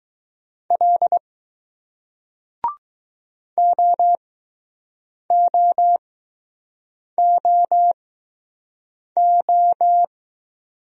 Morse code, heard as a keyed tone, about 20 wpm, spaced as for 6 wpm, 700 Hz. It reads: L OOOO